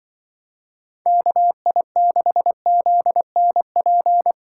Morse code, heard as a keyed tone, about 25 wpm, 700 Hz. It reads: KI6ZNP
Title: Speed Racing - USA 2x3 Call Signs 20wpm